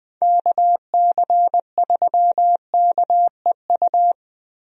Morse code, heard as a keyed tone, about 20 wpm, 700 Hz. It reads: KC3KEU
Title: USA Call Signs - 6 Characters - Repeated 3x - 20wpm